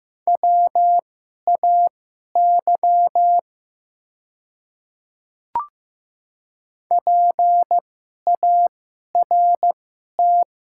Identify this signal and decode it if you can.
Morse code, heard as a keyed tone, about 15 wpm, spaced as for 11 wpm, 700 Hz. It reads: WAY PART
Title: 2x Character Spacing - Top 200 Words - 15wpm